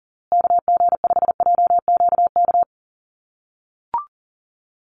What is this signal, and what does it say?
Morse code, heard as a keyed tone, about 37 wpm, 700 Hz. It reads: KG5JQK
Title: USA Call Signs Repeated 2x 40wpm